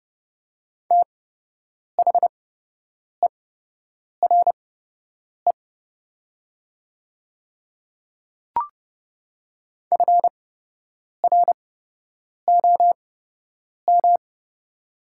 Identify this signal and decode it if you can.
Morse code, heard as a keyed tone, about 30 wpm, spaced as for 8 wpm, 700 Hz. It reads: THERE FROM
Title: 8x Character Spacing - Top 100 Words - 30wpm